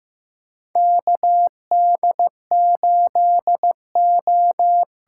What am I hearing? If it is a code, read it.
Morse code, heard as a keyed tone, about 15 wpm, 700 Hz. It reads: KD8O
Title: USA Call Signs - 6 Characters - 8x Character Spacing - 15wpm